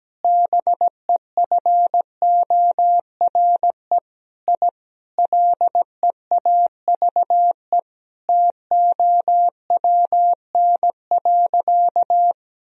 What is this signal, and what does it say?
Morse code, heard as a keyed tone, about 17 wpm, 700 Hz. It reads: BEFORE I LEAVE TOWN.
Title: Sentences from Top 1000 Words - Review Part 2 - 17wpm